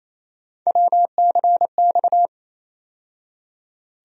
Morse code, heard as a keyed tone, about 28 wpm, 700 Hz. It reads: WCX
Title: USA Call Sign Postfixes - 3 Letters 28wpm